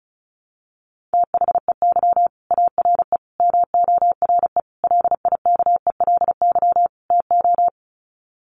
Morse code, heard as a keyed tone, about 34 wpm, 700 Hz. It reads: THEY ARE MORE LIKELY TO
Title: Sets of 5 Words Encore2 35wpm